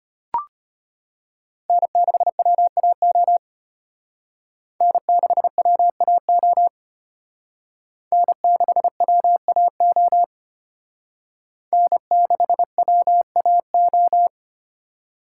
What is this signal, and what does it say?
Morse code, heard as a keyed tone, about 29 wpm, 700 Hz. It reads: N6WAO N6WAO N6WAO N6WAO
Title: Speed Racing - USA Call Signs 25wpm